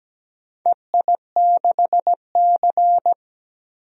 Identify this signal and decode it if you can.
Morse code, heard as a keyed tone, about 17 wpm, 700 Hz. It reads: EI6C
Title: DXCC-100 Call Signs - 4 Characters - Repeated 2x - 17wpm